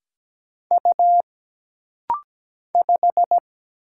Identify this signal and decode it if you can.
Morse code, heard as a keyed tone, about 17 wpm, 700 Hz. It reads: U 5
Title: Single Letter-Number Alt - Rapid-Fire 17wpm